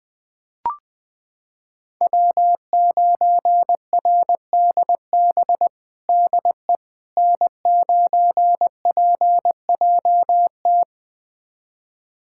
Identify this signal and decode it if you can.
Morse code, heard as a keyed tone, about 21 wpm, 700 Hz. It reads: W9RDB DE N9PJT
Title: USA Callsign DE Callsign - 5 Characters - 20wpm